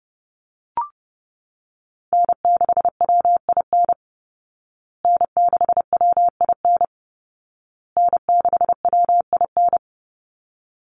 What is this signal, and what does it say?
Morse code, heard as a keyed tone, about 30 wpm, 700 Hz. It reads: N6WIN N6WIN N6WIN
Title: Call Signs Repeated 2x - 30wpm